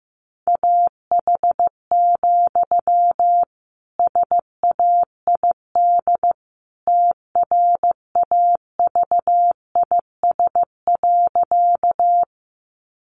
Morse code, heard as a keyed tone, about 15 wpm, 700 Hz. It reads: AH, SAID TRAVIS.
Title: A Sound Of Thunder 15wpm